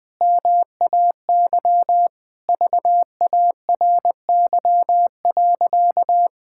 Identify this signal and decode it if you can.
Morse code, heard as a keyed tone, about 20 wpm, 700 Hz. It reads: MAY VARY.
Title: Sentences with 6-Word Phrases - Spoken Beginning - 20wpm